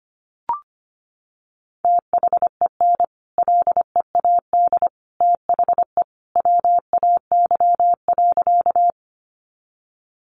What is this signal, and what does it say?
Morse code, heard as a keyed tone, about 25 wpm, 700 Hz. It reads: THEN LEAD THE WAY.
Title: Sentences from Top 500 Words 25wpm